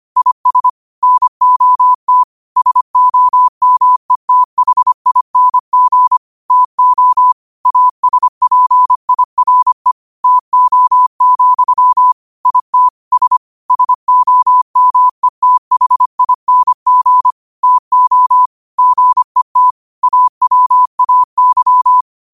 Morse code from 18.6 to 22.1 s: GET AWAY

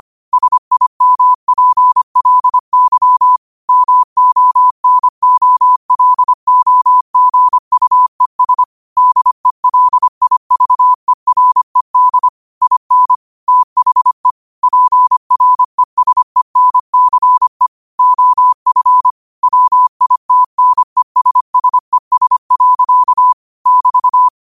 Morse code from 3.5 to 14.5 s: MONOLOGUES DELIVERED IN THE

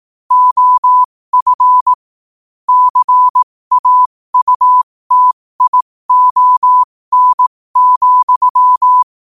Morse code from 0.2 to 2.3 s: OF

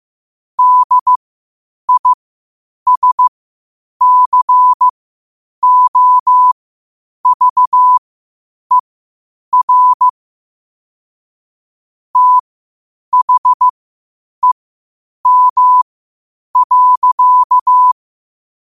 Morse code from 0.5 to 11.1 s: DISCOVER

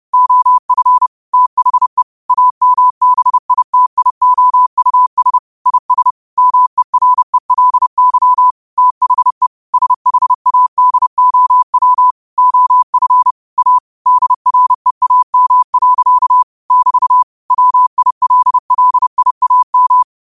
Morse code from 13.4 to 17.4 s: A DREAM. <BT>

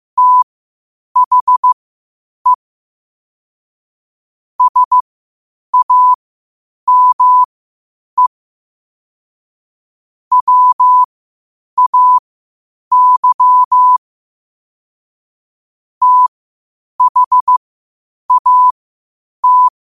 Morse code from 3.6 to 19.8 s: SAME WAY THAT

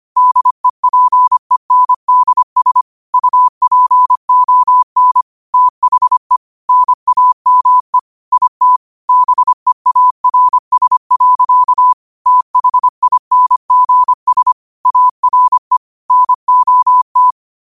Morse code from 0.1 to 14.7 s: DEPENDS UPON THE NAME IT BEARS. THINGS

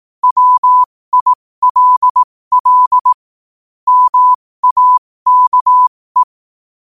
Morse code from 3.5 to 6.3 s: MAKE

